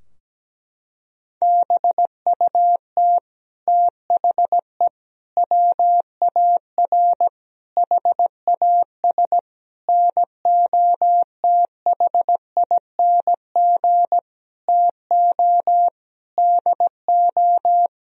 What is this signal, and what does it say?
Morse code, heard as a keyed tone, about 17 wpm, 700 Hz. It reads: BUT THE WAR HAS NOTHING TO DO
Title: Sentences from Top 1000 Words - Review Part 2 - 17wpm